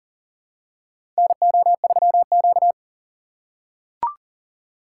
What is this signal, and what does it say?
Morse code, heard as a keyed tone, about 40 wpm, 700 Hz. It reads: NO3Q
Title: USA Call Signs - 4 Characters - 8x Character Spacing - 40wpm